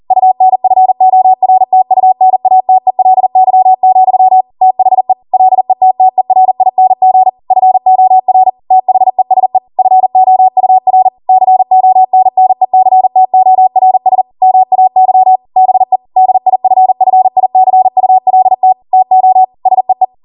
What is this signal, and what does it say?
Morse code, heard as a keyed tone, about 40 wpm, 750 Hz. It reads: UNFORTUNATELY, THE LETTERING FOR THESE FOUR CONNECTORS MAY BE DIFFICULT TO SEE